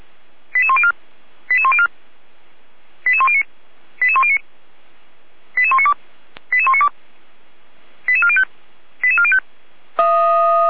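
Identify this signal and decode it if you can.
Five-tone selective call (ZVEI), heard as quick runs of five tones, tones of about 70 ms. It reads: 88104 88104 88109 88109 88192 88192 88405 88405
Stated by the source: Received on SDR by feuerwehr-orlamuende.org.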